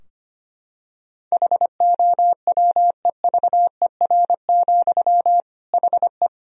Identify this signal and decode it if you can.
Morse code, heard as a keyed tone, about 25 wpm, 700 Hz. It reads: HOWEVER, HE
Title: Sentences from Top 1000 Words - Review Part 2 - 25wpm